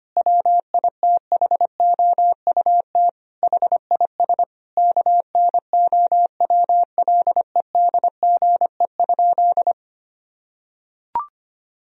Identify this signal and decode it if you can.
Morse code, heard as a keyed tone, about 25 wpm, 700 Hz. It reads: WITHOUT HIS KNOWLEDGE?
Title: Sentences from Top 1000 Words - Review Part 3 - 25wpm